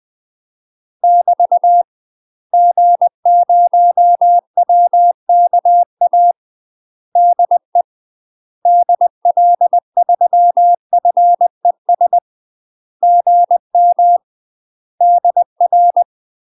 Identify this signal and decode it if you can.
Morse code, heard as a keyed tone, about 20 wpm, 700 Hz. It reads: <BT> G0WKA DE DL3FES GM DR
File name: UK MASTER QSOs TRAINING / 20WPM / UK MASTER QSOs 0026